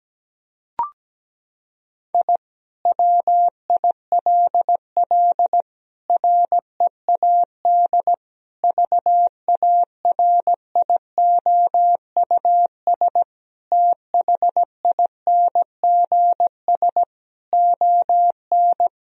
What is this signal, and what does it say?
Morse code, heard as a keyed tone, about 17 wpm, 700 Hz. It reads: I WILL READ VARIOUS THINGS ON
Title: Sentences from Top 1000 Words - Review Part 4 - 17wpm